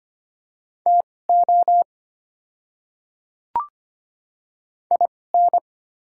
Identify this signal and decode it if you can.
Morse code, heard as a keyed tone, about 25 wpm, spaced as for 18 wpm, 700 Hz. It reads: TO IN
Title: 2x Character Spacing - Top 300 Words - 25wpm